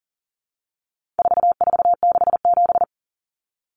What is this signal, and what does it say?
Morse code, heard as a keyed tone, about 40 wpm, 700 Hz. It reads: 4467
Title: Numbers - 4-Digit 40wpm